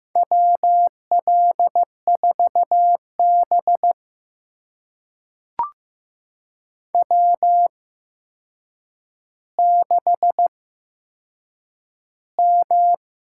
Morse code, heard as a keyed tone, about 15 wpm, spaced as for 7 wpm, 700 Hz. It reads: WL4B W6M
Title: USA Call Signs - 4 Characters - 8x Character Spacing - 15wpm